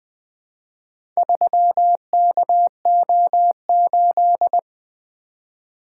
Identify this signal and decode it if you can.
Morse code, heard as a keyed tone, about 21 wpm, 700 Hz. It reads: 3KO8
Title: Four Letter-Number Combinations - 20wpm